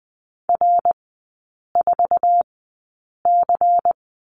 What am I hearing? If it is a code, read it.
Morse code, heard as a keyed tone, about 20 wpm, spaced as for 8 wpm, 700 Hz. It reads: R4C
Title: 121 Lesson 20e - Number 3 - Three Characters 20wpm